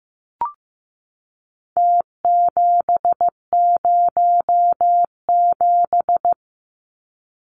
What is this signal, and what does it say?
Morse code, heard as a keyed tone, about 15 wpm, 700 Hz. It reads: T707
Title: Numbers - Contest Style - 4-Digit 15wpm